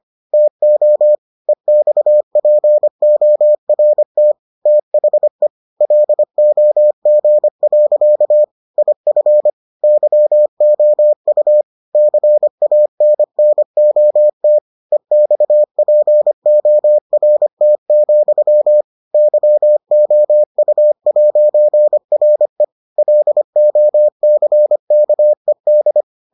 Morse code, heard as a keyed tone, about 25 wpm, 600 Hz. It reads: TO EXPORT THE LOG. IF YOU CANNOT EXPORT, YOU'RE LOCKED